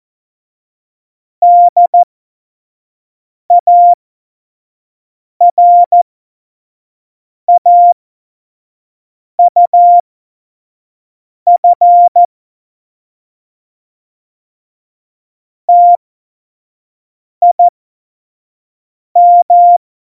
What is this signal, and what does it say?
Morse code, heard as a keyed tone, about 14 wpm, spaced as for 5 wpm, 700 Hz. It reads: DARAUF TIM
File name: Morsekurs / Witze / Witz 21. Mai 2019, 25 F70